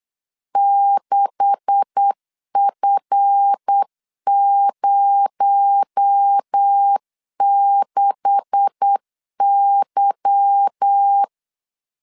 Morse code, heard as a keyed tone, about 8 wpm, 800 Hz. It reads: 6F06Y
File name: LEZIONI / 10 SIGNS / 10C DOT 8wpm 3649